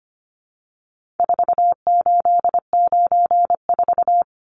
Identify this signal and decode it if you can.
Morse code, heard as a keyed tone, about 25 wpm, 700 Hz. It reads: 4894
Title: Numbers - 4-Digit 25wpm